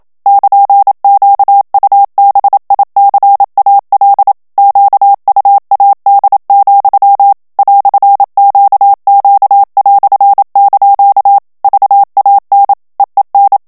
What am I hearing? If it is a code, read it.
Morse code, heard as a keyed tone, about 27 wpm, 800 Hz. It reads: <KN>QUBICAL QUAD, "QQ") VAN EEN
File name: HOLLANDAIS texte clair 1 vitesse 26 mots minute